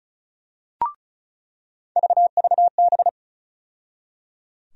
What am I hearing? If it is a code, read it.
Morse code, heard as a keyed tone, about 36 wpm, 700 Hz. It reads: VVB
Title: USA Call Sign Postfixes - 3 Letters 35wpm